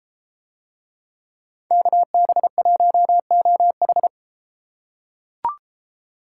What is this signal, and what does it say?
Morse code, heard as a keyed tone, about 32 wpm, 700 Hz. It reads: KB1OH